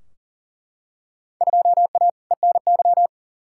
Morse code, heard as a keyed tone, about 40 wpm, 700 Hz. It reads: 2A ENY
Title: Field Day Exchanges - 40wpm